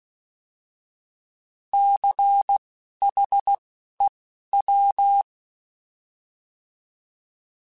Morse code, heard as a keyed tone, about 16 wpm, spaced as for 11 wpm, 800 Hz. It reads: CHEW
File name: Alphabetical / Alphabetical mp3s / Cs @10